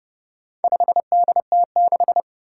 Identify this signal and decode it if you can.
Morse code, heard as a keyed tone, about 31 wpm, 700 Hz. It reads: 5DT6
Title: Four Challenging Letter-Number Combinations 30wpm